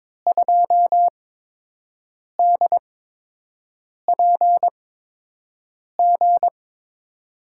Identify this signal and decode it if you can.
Morse code, heard as a keyed tone, about 22 wpm, spaced as for 6 wpm, 700 Hz. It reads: 2DPG
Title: USA Call Signs - 6 Characters - 8x Character Spacing - 22wpm